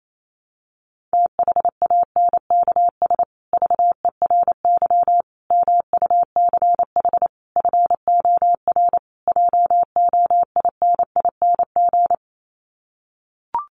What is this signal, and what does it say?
Morse code, heard as a keyed tone, about 28 wpm, 700 Hz. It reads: THANKS VERY MUCH FOR JOINING
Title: Sets of 5 Words Encore2 28wpm